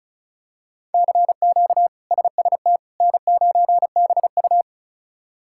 Morse code, heard as a keyed tone, about 34 wpm, 700 Hz. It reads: CQ SST N9BU
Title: K1USN SST QSO Practice - 35wpm